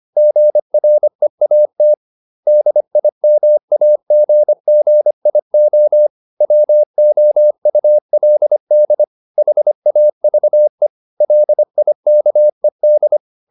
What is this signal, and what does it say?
Morse code, heard as a keyed tone, about 25 wpm, 600 Hz. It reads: GREAT DIMAGGIO WOULD HAVE LIKED